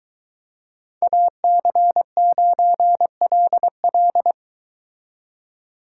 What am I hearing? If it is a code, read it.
Morse code, heard as a keyed tone, about 23 wpm, 700 Hz. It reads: AC9LL